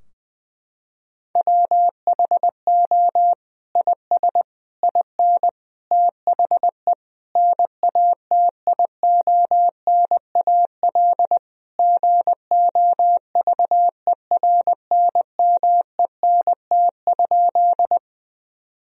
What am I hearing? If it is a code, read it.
Morse code, heard as a keyed tone, about 21 wpm, 700 Hz. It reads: WHO IS IN THE NATIONAL GOVERNMENT?
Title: Sentences from Top 1000 Words - Review Part 4 - 20wpm